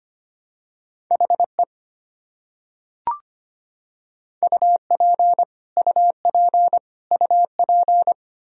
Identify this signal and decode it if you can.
Morse code, heard as a keyed tone, about 25 wpm, 700 Hz. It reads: HE UP UP UP
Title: Top 100 Words Repeated 25wpm